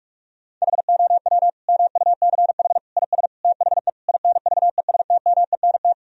Morse code, heard as a keyed tone, about 46 wpm, 700 Hz. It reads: HOW MUCH IS THE INVESTMENT
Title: Sentences from Top 1000 Words - Review Part 4 - 45wpm